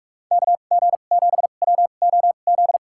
Morse code, heard as a keyed tone, about 46 wpm, spaced as for 33 wpm, 700 Hz. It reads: KG7WOZ
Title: USA Call Signs - 2x Character Spacing - 45wpm